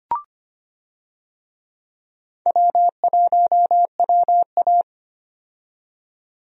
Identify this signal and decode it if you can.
Morse code, heard as a keyed tone, about 25 wpm, 700 Hz. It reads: W1WA
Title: USA Call Signs - 4 Characters - 25wpm